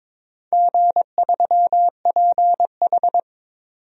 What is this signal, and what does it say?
Morse code, heard as a keyed tone, about 21 wpm, 700 Hz. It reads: G3PH
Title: DXCC-100 Call Signs - 4 Characters - 22wpm